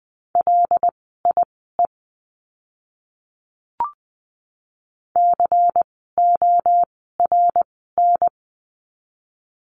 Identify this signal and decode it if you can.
Morse code, heard as a keyed tone, about 20 wpm, spaced as for 14 wpm, 700 Hz. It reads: LIE CORN
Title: 094 Lesson 16f - Letter c - Words 20wpm